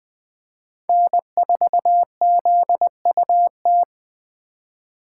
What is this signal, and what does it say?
Morse code, heard as a keyed tone, about 20 wpm, 700 Hz. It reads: N4ZUT
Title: USA Call Signs - 5 Characters - 4x Character Spacing - 20wpm